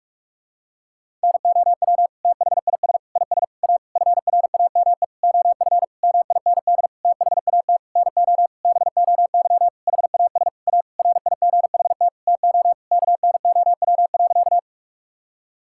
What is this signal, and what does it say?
Morse code, heard as a keyed tone, about 41 wpm, 700 Hz. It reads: NOW THIS IS A FRAME OF MIND THAT NO BOY HAS A RIGHT TO KNOW.
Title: Sentences from Top 1000 Words - Review Part 6 - 45wpm